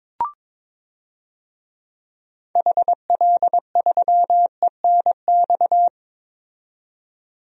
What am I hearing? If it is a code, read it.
Morse code, heard as a keyed tone, about 22 wpm, 700 Hz. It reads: HL3ENX
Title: DXCC-100 Call Signs - 6 Characters - 22wpm